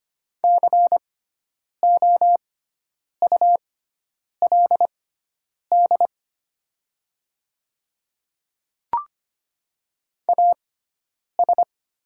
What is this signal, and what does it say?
Morse code, heard as a keyed tone, about 25 wpm, spaced as for 9 wpm, 700 Hz. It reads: COULD AS
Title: 6x Character Spacing - Top 300 Words - 25wpm